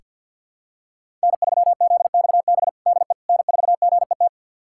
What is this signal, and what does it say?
Morse code, heard as a keyed tone, about 50 wpm, 700 Hz. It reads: N3ZXB DE N4GET
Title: USA Callsign DE Callsign - 5 Characters - 50wpm